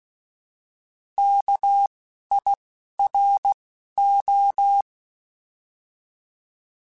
Morse code, heard as a keyed tone, about 16 wpm, 800 Hz. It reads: K I R O
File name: Alphabetical / Alphabetical mp3s / Ks @10